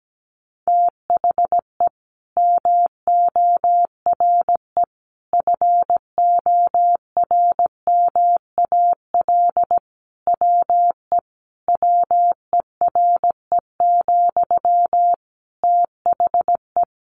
Morse code, heard as a keyed tone, about 17 wpm, 700 Hz. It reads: THE MORE FORMAL WE WERE, THE
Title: Sentences from Top 1300 Words 17wpm